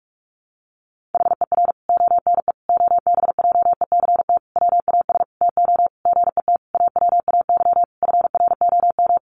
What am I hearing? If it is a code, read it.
Morse code, heard as a keyed tone, about 44 wpm, 700 Hz. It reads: HER ONE OBJECT WAS TO GET AWAY FROM
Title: Sentences from Top 1500 Words 45wpm